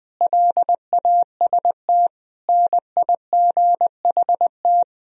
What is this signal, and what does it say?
Morse code, heard as a keyed tone, about 20 wpm, 700 Hz. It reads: LAST NIGHT
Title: Sentences from Top 1000 Words - Review Part 4 - 20wpm